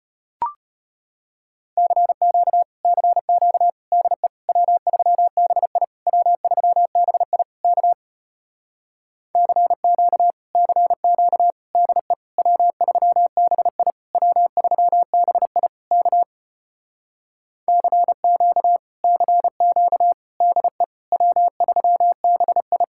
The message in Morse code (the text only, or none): CQ CQ DE W3BI W3BI K CQ CQ DE W3BI W3BI K CQ CQ DE W3BI